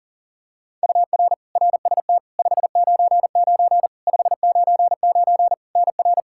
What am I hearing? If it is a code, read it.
Morse code, heard as a keyed tone, about 40 wpm, 700 Hz. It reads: UR RST 599 599 NR
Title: K1USN SST QSO Practice 40wpm